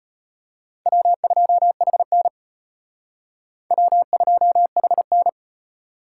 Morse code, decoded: W2HN W2HN

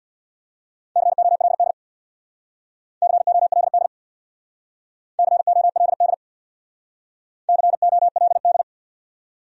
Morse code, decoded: COLD COLD COLD COLD

